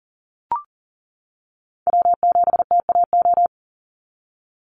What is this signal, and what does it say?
Morse code, heard as a keyed tone, about 41 wpm, 700 Hz. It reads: W7TAO
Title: NA SOTA Chaser Callsigns 40wpm